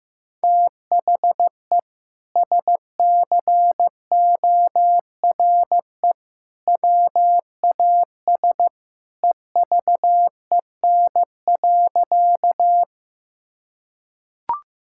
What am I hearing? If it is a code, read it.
Morse code, heard as a keyed tone, about 15 wpm, 700 Hz. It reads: THE SCORE WAS EVEN.
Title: Sentences from Top 1600 Words 15wpm